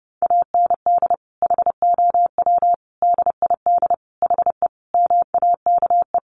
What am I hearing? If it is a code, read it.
Morse code, heard as a keyed tone, about 30 wpm, 700 Hz. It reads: AND HOW DID HE MAKE
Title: Sentences from Top 400 Words 30wpm